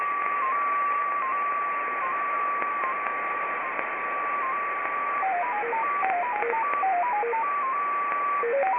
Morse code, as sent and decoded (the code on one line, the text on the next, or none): none